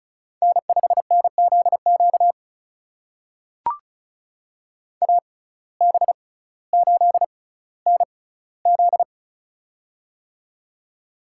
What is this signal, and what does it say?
Morse code, heard as a keyed tone, about 36 wpm, spaced as for 18 wpm, 700 Hz. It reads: N5NZQ AB8NZ
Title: USA Call Signs - 5 Characters - 6x Character Spacing - 35wpm